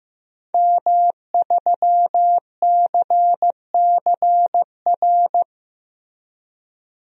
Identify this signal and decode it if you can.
Morse code, heard as a keyed tone, about 15 wpm, 700 Hz. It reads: M3CCR
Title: DXCC-100 Call Signs - 5 Characters - Repeated 2x - 15wpm